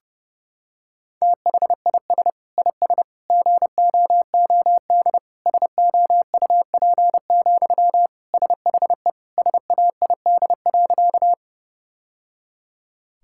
Morse code, heard as a keyed tone, about 30 wpm, 700 Hz. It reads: THIS IS GOOD SOUP, SHE SAID.
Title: Sentences from Top 1000 Words - Review Part 6 - 30wpm